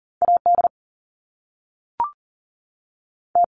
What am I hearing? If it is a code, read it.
Morse code, heard as a keyed tone, about 40 wpm, 700 Hz. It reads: AD T
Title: Top Two Letter Combinations in Words 40wpm